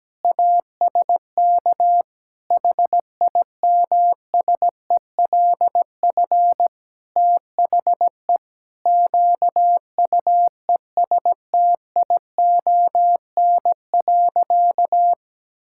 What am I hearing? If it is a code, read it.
Morse code, heard as a keyed tone, about 17 wpm, 700 Hz. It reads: ASK HIMSELF THE QUESTION.
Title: Sentences from Top 1600 Words 17wpm